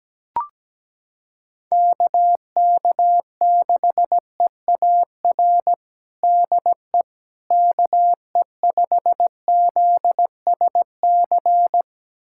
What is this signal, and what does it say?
Morse code, heard as a keyed tone, about 17 wpm, 700 Hz. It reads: KK6EAR DE KE5ZSC
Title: USA Callsign DE Callsign 17wpm